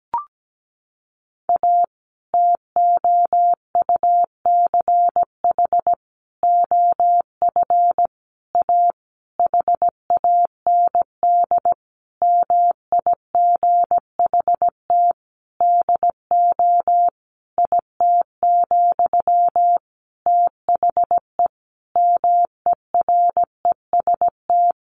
Morse code, heard as a keyed tone, about 17 wpm, 700 Hz. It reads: A TOUCH OF A HAND MIGHT DO IT, THE MEREST
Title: A Sound Of Thunder 17wpm